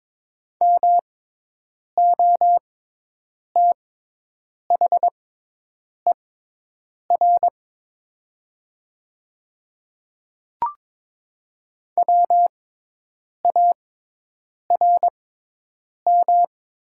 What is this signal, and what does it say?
Morse code, heard as a keyed tone, about 22 wpm, spaced as for 8 wpm, 700 Hz. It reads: MOTHER WARM
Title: 6x Character Spacing - Top 500 Words - 22wpm